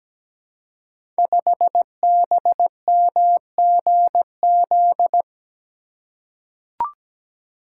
Morse code, heard as a keyed tone, about 17 wpm, 700 Hz. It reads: 5BMGZ
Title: Code Groups - 5 Letter:Numbers - 17wpm